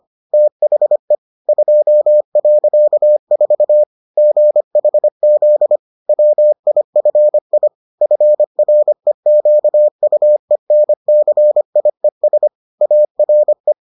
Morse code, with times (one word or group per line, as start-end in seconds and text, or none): 0.3-1.1 THE
1.5-3.8 2.4
4.2-5.8 GHZ
6.1-7.7 WIFI
8.0-12.5 FREQUENCIES
12.8-13.7 ARE